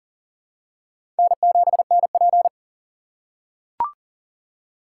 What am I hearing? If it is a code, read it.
Morse code, heard as a keyed tone, about 39 wpm, 700 Hz. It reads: N7NP
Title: USA Call Signs - 4 Characters - 3x Character Spacing - 40wpm